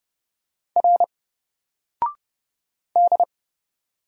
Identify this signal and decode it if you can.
Morse code, heard as a keyed tone, about 30 wpm, 700 Hz. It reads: R D
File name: Single-Letter-Number-30wpm